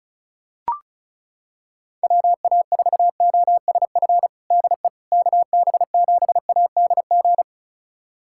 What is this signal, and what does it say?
Morse code, heard as a keyed tone, about 36 wpm, 700 Hz. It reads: WA4OSF DE KB7ADG
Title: USA Callsign DE Callsign - 6 Characters - 35wpm